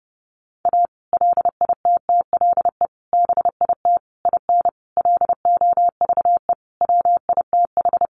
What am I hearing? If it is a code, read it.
Morse code, heard as a keyed tone, about 30 wpm, 700 Hz. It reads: A LITTLE BIT IN LOVE WITH
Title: Sentences from Top 900 Words 30wpm